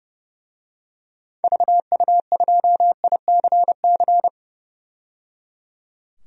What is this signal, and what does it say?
Morse code, heard as a keyed tone, about 29 wpm, 700 Hz. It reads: VU2ICC
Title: DXCC-100 Call Signs - 6 Characters - 30wpm